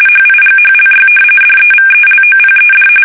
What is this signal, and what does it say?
RTTY, about 45.45 baud. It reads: HIGHLY DEVELOPED
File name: fsk850 / usnavy5